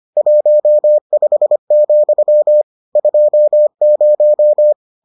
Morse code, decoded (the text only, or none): 15, 20